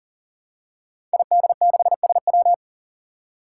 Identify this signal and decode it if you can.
Morse code, heard as a keyed tone, about 40 wpm, 700 Hz. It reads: ID6SW
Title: Five Challenging Letter-Number Combinations 40wpm